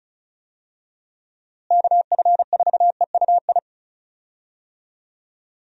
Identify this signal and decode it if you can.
Morse code, heard as a keyed tone, about 36 wpm, 700 Hz. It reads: KF4EUI
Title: USA Call Signs - 6 Characters - Repeated 2x - 35wpm